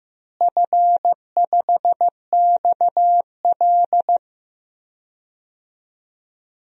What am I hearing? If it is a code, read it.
Morse code, heard as a keyed tone, about 15 wpm, 700 Hz. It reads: F5XL
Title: DXCC-100 Call Signs - 4 Characters - Repeated 3x - 15wpm